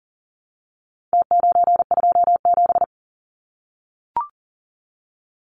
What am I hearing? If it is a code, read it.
Morse code, heard as a keyed tone, about 40 wpm, 700 Hz. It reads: T927
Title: Numbers - Contest Style - 4-Digit 40wpm